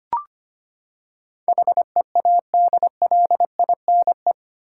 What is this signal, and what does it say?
Morse code, heard as a keyed tone, about 25 wpm, 700 Hz. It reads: HEADLINE
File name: Eight Letter Words - 25wpm